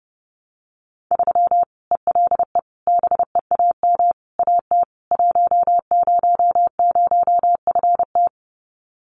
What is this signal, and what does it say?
Morse code, heard as a keyed tone, about 30 wpm, 700 Hz. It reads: 3 ELE BEAM AT 100FT
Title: Common QSO Phrases 30wpm